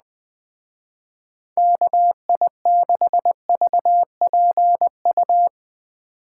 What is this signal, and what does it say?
Morse code, heard as a keyed tone, about 20 wpm, 700 Hz. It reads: KI6VPU
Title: Speed Racing - USA 2x3 Call Signs 17wpm